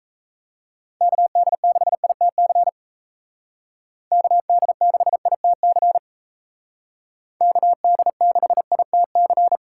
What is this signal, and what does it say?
Morse code, heard as a keyed tone, about 34 wpm, 700 Hz. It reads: KD6ITC KD6ITC KD6ITC
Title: Speed Racing - USA Call Signs 28wpm